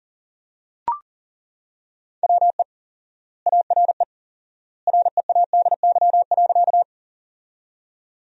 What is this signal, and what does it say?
Morse code, heard as a keyed tone, about 40 wpm, 700 Hz. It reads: WE ARE READY.
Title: 4x Word-Spacing - Sentences from Top 400 Words - 40wpm